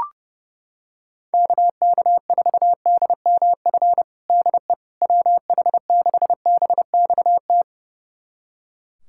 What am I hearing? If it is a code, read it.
Morse code, heard as a keyed tone, about 30 wpm, 700 Hz. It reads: KK4DMF DE WH6BXT
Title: USA Callsign DE Callsign 30wpm